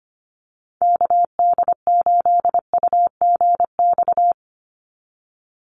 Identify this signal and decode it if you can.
Morse code, heard as a keyed tone, about 25 wpm, 700 Hz. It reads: KD8UGX
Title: USA Call Signs 25wpm